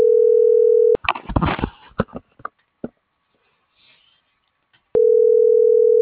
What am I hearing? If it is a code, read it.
Ringback tone (North America), 440 + 480 Hz.
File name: CENTRAL SOTOVA / out carlos.sotova - .59